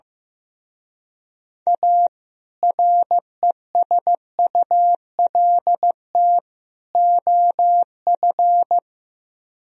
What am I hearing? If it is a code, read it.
Morse code, heard as a keyed tone, about 15 wpm, 700 Hz. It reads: A RESULT OF
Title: Speed-Racing - Sets of 3 Words 15wpm